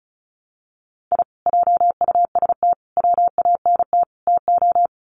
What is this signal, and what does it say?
Morse code, heard as a keyed tone, about 34 wpm, 700 Hz. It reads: I JUST WANT TO
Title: Sets of 4 Words 35wpm